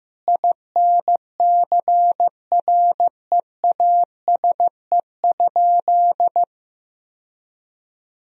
Morse code, heard as a keyed tone, about 16 wpm, 700 Hz. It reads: INCREASE?
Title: Sentences from Top 1000 Words - Review Part 4 - 15wpm